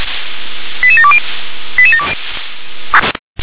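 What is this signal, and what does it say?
Five-tone selective call (ZVEI), heard as quick runs of five tones, tones of about 70 ms. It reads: 88720
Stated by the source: ZVEI five-tone call 88631 received by SDR.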